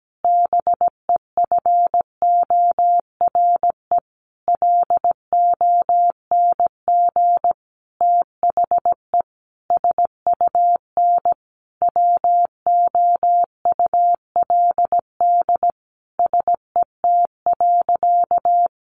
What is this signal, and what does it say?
Morse code, heard as a keyed tone, about 17 wpm, 700 Hz. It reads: BEFORE LONG THE SUN WOULD SET.
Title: Sentences from Top 400 Words 17wpm